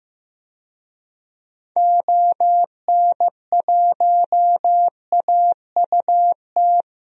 Morse code, heard as a keyed tone, about 15 wpm, 700 Hz. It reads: ON1AUT
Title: DXCC-100 Call Signs - 6 Characters - 15wpm